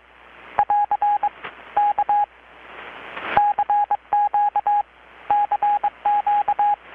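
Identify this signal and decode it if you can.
Morse code, heard as a keyed tone, about 22 wpm, 820 Hz. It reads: <AR> K CQ CQ
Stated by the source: Morse from KPH's Night of Nights XX maritime radiotelegraphy event, received in Chiapas, Mexico about 3,800 km away on a Kenwood R-600 with a grounded 10 m long-wire antenna.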